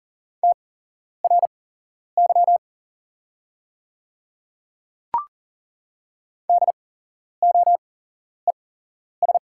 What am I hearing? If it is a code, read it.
Morse code, heard as a keyed tone, about 40 wpm, spaced as for 11 wpm, 690 Hz. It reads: TRY DOES
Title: 8x Character Spacing - Top 200 Words - 40wpm